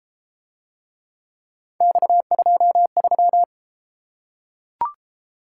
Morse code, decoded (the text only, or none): X23